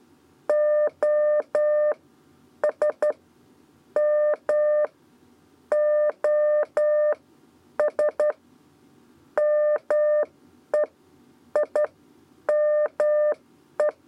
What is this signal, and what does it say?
Morse code, heard as a keyed tone, about 11 wpm, spaced as for 8 wpm, 600 Hz. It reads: OS M OS MEIME
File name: Lezione trasmessa 2 lez 2 exe